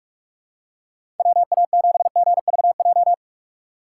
Morse code, heard as a keyed tone, about 46 wpm, 700 Hz. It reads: WA7GVJ
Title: USA Call Signs - 6x Character Spacing - 45wpm